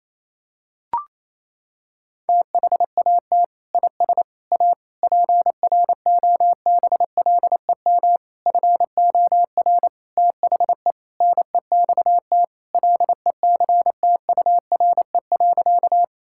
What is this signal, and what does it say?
Morse code, heard as a keyed tone, about 28 wpm, 700 Hz. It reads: THAT IS A PROBLEM FOR THE NEXT LECTURE.
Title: Sentences from Top 1600 Words 28wpm